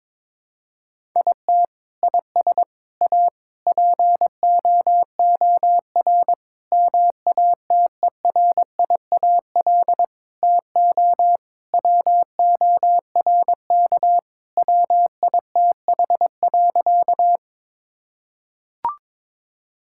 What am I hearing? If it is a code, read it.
Morse code, heard as a keyed tone, about 22 wpm, 700 Hz. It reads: IT IS A POOR MATERIAL TO WORK WITH.
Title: Sentences from Top 1000 Words - Review Part 5 - 22wpm